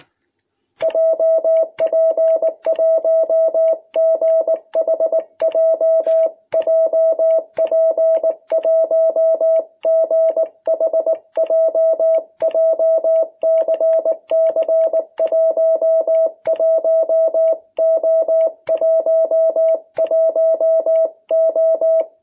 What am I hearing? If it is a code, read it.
Morse code, heard as a keyed tone, about 19 wpm, 660 Hz. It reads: JP1GHJ JP1GHJ JCC11O11O
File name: jh8cbh / 2021cw